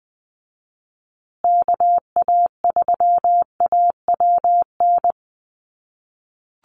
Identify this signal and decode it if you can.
Morse code, heard as a keyed tone, about 20 wpm, 700 Hz. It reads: KA3AWN
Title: USA Call Signs Repeated 2x 20wpm